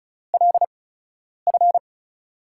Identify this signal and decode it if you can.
Morse code, heard as a keyed tone, about 36 wpm, spaced as for 10 wpm, 700 Hz. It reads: LF
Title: USA Call Signs - 6 Characters - 8x Character Spacing - 35wpm